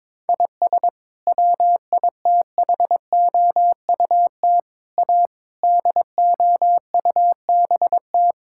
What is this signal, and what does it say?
Morse code, heard as a keyed tone, about 22 wpm, 700 Hz. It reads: IS WITHOUT A DOUBT
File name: practice / 22wpm / Sets of 7 Words - 22wpm